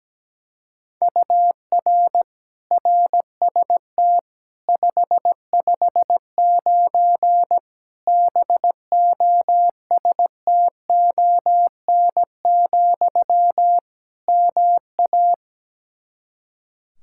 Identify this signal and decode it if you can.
Morse code, heard as a keyed tone, about 17 wpm, 700 Hz. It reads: UR RST 559 BOSTON, MA
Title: K1USN SST QSO Practice 17wpm